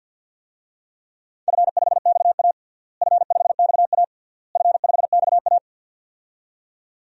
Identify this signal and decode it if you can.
Morse code, heard as a keyed tone, about 47 wpm, 700 Hz. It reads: F5XA F5XA F5XA
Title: DXCC-100 Call Signs - 4 Characters - Repeated 3x - 50wpm